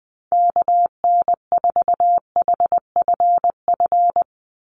Morse code, heard as a keyed tone, about 20 wpm, 700 Hz. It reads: KN4HFF
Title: USA Call Signs 20wpm